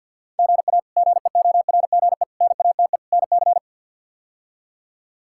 Morse code, heard as a keyed tone, about 53 wpm, 700 Hz. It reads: GA GEORGE NATE NC
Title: K1USN SST QSO Practice - 50wpm